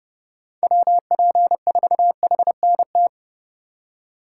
Morse code, heard as a keyed tone, about 32 wpm, 700 Hz. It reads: WP4HNT
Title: USA Call Signs - 6 Characters - 2x Character Spacing - 30wpm